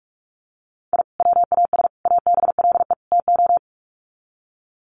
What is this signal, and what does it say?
Morse code, heard as a keyed tone, about 44 wpm, 700 Hz. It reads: I WAS ABLE TO
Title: Sets of 4 Words 45wpm